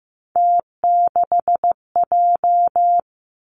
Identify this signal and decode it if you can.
Morse code, heard as a keyed tone, about 15 wpm, 700 Hz. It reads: T6J
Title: Three Challenging Letter-Number Combinations 15wpm